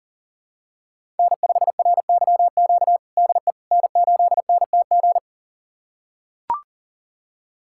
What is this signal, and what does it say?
Morse code, heard as a keyed tone, about 40 wpm, 700 Hz. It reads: N5RYQ DE N8NTG
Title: USA Callsign DE Callsign - 5 Characters - 40wpm